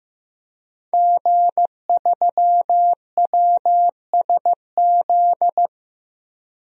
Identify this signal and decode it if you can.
Morse code, heard as a keyed tone, about 15 wpm, 700 Hz. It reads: G3WSZ
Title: DXCC-100 Call Signs - 5 Characters - Repeated 3x - 15wpm